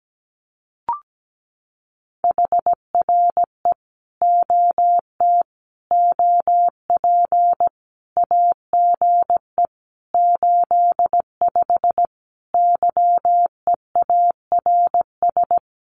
Morse code, heard as a keyed tone, about 17 wpm, 700 Hz. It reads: HRE OT OP AGE 85 YEARS
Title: Common QSO Phrases 17wpm